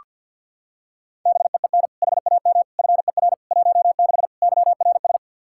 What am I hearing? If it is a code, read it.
Morse code, heard as a keyed tone, about 47 wpm, 700 Hz. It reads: BEEN HAM FER 16 YRS
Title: Practice QSO - Exchange 3 - More Info - 50wpm